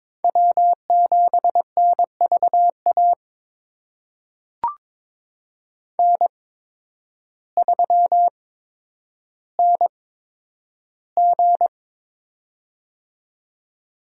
Morse code, heard as a keyed tone, about 22 wpm, spaced as for 10 wpm, 700 Hz. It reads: W7NVA N3NG
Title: USA Call Signs - 8x Character Spacing - 22wpm